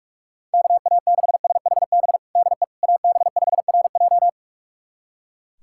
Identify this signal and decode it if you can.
Morse code, heard as a keyed tone, about 46 wpm, 700 Hz. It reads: KA6SHB DE AB5RJ
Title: USA Callsign DE Callsign 45wpm